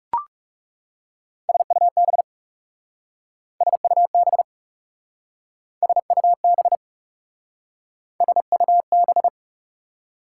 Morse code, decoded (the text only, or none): SUB SUB SUB SUB